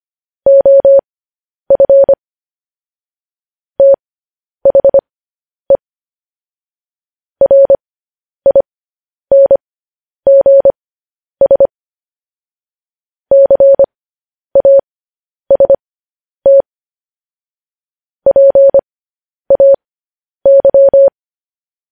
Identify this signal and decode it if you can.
Morse code, heard as a keyed tone, about 25 wpm, spaced as for 10 wpm, 550 Hz. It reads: OF THE RINGS CAST PAY
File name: podcast / News headlines-1x-25x10